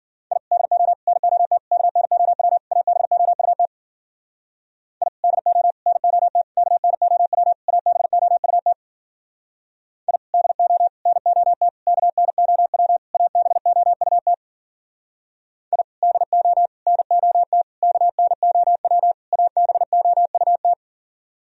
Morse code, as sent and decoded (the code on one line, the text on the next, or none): .. / -.. --- / -. --- - / -.- -. --- .-- / .- -... --- ..- - / .. / -.. --- / -. --- - / -.- -. --- .-- / .- -... --- ..- - / .. / -.. --- / -. --- - / -.- -. --- .-- / .- -... --- ..- - / .. / -.. --- / -. --- - / -.- -. --- .-- / .- -... --- ..- -
I DO NOT KNOW ABOUT I DO NOT KNOW ABOUT I DO NOT KNOW ABOUT I DO NOT KNOW ABOUT